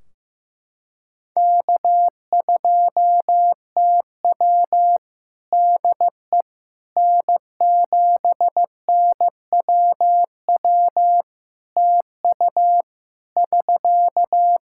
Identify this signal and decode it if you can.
Morse code, heard as a keyed tone, about 15 wpm, 700 Hz. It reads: K2TW DE N7NWW TU <SK>
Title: K1USN SST QSO Practice 15wpm